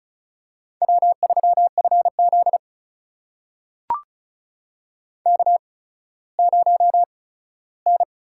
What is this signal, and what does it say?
Morse code, heard as a keyed tone, about 36 wpm, spaced as for 17 wpm, 700 Hz. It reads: W3FZ K0N
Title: USA Call Signs - 4 Characters - 8x Character Spacing - 35wpm